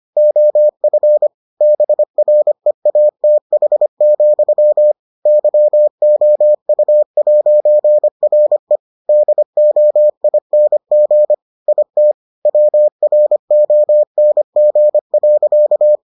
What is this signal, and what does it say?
Morse code, heard as a keyed tone, about 25 wpm, 600 Hz. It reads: OF BREATH, YOU'RE DOING IT WRONG.